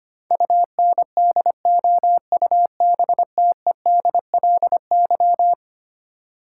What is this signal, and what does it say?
Morse code, heard as a keyed tone, about 25 wpm, 700 Hz. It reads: UNDOUBTEDLY
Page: morse-code-ninja-practice-25wpm